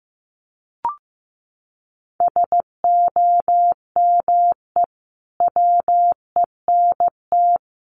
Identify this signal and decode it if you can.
Morse code, heard as a keyed tone, about 15 wpm, 700 Hz. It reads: SOME WENT
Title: Sentences from Top 1100 Words 15wpm